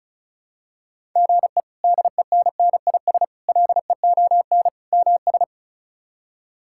Morse code, heard as a keyed tone, about 36 wpm, 700 Hz. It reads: GE DENNIS LEON MS
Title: K1USN SST QSO Practice - 35wpm